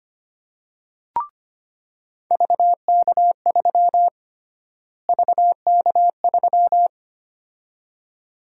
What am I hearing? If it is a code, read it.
Morse code, heard as a keyed tone, about 25 wpm, 700 Hz. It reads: VK3 VK3
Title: USA Call Sign Prefixes - Repeated 2x - 25wpm